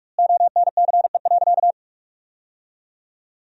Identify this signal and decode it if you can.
Morse code, heard as a keyed tone, about 42 wpm, 700 Hz. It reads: ONCE.
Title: 2x Word-Spacing - Sentences from Top 300 Words - 45wpm